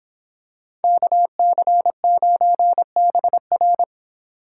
Morse code, heard as a keyed tone, about 26 wpm, 700 Hz. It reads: KC9BR
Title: Speed Racing - USA 2x2 Call Signs 22wpm